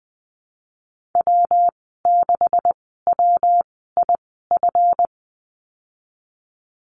Morse code, heard as a keyed tone, about 20 wpm, spaced as for 14 wpm, 700 Hz. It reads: W6WIF
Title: 161 Lesson 26g - Letter g - Call Signs 20wpm